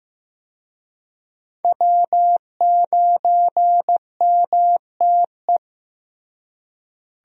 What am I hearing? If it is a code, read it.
Morse code, heard as a keyed tone, about 15 wpm, 700 Hz. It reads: W9MTE
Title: USA Call Signs - 5 Characters - Repeated 2x - 15wpm